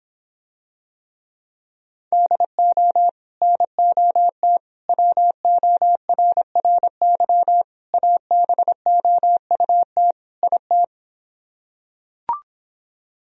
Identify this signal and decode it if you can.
Morse code, heard as a keyed tone, about 26 wpm, 700 Hz. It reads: DO NOT WORRY ABOUT IT